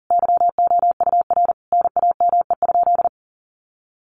Morse code, heard as a keyed tone, about 40 wpm, 700 Hz. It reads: YOUR NAME?
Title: Sentences from Top 1200 Words 40wpm